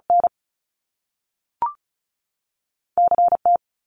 Morse code, heard as a keyed tone, about 36 wpm, 700 Hz. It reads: N CT
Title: US States 35wpm